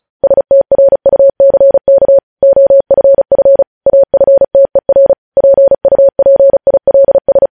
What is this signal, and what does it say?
Morse code, heard as a keyed tone, about 36 wpm, 550 Hz. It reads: STRUCK OFF AFTER PUPILS
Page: news headlines in morse code at 35 wpm